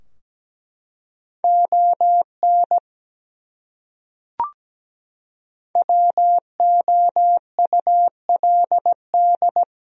Morse code, heard as a keyed tone, about 17 wpm, 700 Hz. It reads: ON WOULD
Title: Top 100 Words 17wpm